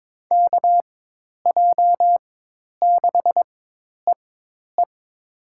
Morse code, heard as a keyed tone, about 21 wpm, spaced as for 10 wpm, 700 Hz. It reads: KJ6EE
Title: USA Call Signs - 4x Character Spacing - 22wpm